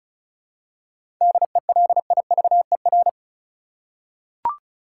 Morse code, heard as a keyed tone, about 36 wpm, 700 Hz. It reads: DELIVER
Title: Seven Letter Words - 35wpm